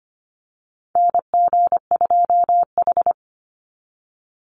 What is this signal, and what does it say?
Morse code, heard as a keyed tone, about 25 wpm, 700 Hz. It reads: NG2H
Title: Call Signs Repeated 3x - 25wpm